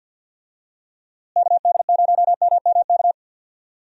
Morse code, heard as a keyed tone, about 53 wpm, 700 Hz. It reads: KD0MMK
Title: USA Call Signs - 6 Characters - Repeated 3x - 50wpm